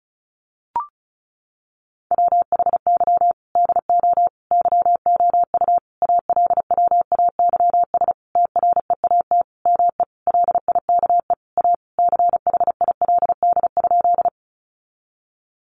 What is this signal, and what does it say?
Morse code, heard as a keyed tone, about 34 wpm, 700 Hz. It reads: WHY DO YOU ALWAYS TREAT ME LIKE A CHILD?
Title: Sentences from Top 1300 Words 35wpm